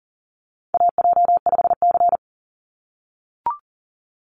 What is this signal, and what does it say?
Morse code, heard as a keyed tone, about 40 wpm, 700 Hz. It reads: AJ5C
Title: NA SOTA Chaser Callsigns 40wpm